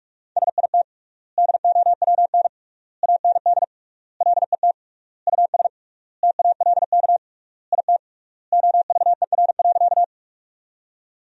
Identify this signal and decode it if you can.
Morse code, heard as a keyed tone, about 46 wpm, 700 Hz. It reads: SIT DOWN AND LET US TALK IT OVER.
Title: 3x Word-Spacing - Sentences from Top 500 Words - 45wpm